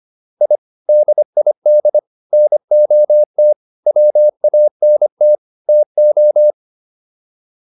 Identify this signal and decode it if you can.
Morse code, heard as a keyed tone, about 25 wpm, 600 Hz. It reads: I DID NOT WANT TO